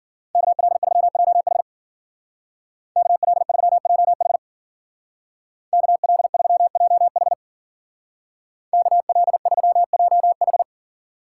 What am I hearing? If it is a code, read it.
Morse code, heard as a keyed tone, about 50 wpm, 700 Hz. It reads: KL3JH KL3JH KL3JH KL3JH
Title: Speed Racing - USA Call Signs 40wpm